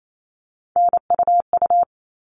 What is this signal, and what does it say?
Morse code, heard as a keyed tone, about 28 wpm, 700 Hz. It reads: NUU
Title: All Cut Numbers - 3-Digit 28wpm